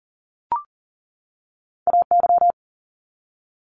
Morse code, heard as a keyed tone, about 42 wpm, 700 Hz. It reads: AY
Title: Multi-letter Phonemes 40wpm